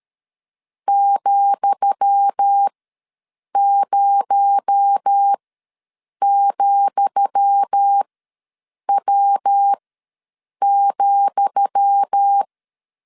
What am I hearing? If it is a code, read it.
Morse code, heard as a keyed tone, about 13 wpm, spaced as for 7 wpm, 800 Hz. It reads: ,0,W,
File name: LEZIONI / 10 SIGNS / 10D COMA ws 12wpm 3618